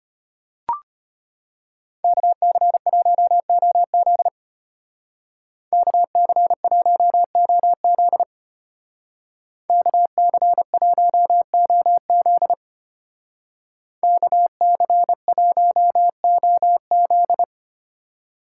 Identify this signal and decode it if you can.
Morse code, heard as a keyed tone, about 33 wpm, 700 Hz. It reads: KC1OZ KC1OZ KC1OZ KC1OZ